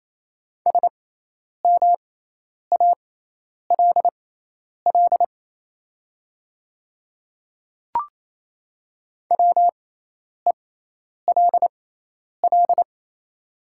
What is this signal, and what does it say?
Morse code, heard as a keyed tone, about 27 wpm, spaced as for 10 wpm, 690 Hz. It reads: SMALL WELL
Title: 6x Character Spacing - Top 300 Words - 28wpm